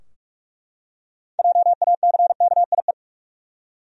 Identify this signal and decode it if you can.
Morse code, heard as a keyed tone, about 41 wpm, 700 Hz. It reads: JACKIE
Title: USA Common Names - 45wpm